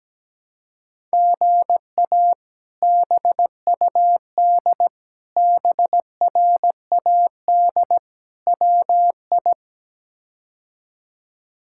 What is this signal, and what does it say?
Morse code, heard as a keyed tone, about 17 wpm, 700 Hz. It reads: GA BUD BRAD WI
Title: K1USN SST QSO Practice - 17wpm